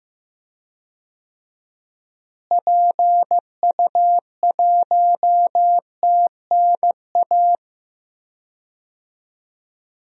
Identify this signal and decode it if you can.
Morse code, heard as a keyed tone, about 15 wpm, 700 Hz. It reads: PU1TNA
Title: DXCC-100 Call Signs - 6 Characters - Repeated 2x - 15wpm